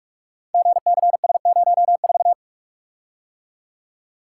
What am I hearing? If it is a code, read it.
Morse code, heard as a keyed tone, about 47 wpm, 700 Hz. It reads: GCS04
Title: Code Groups - 5 Letter:Numbers - 45wpm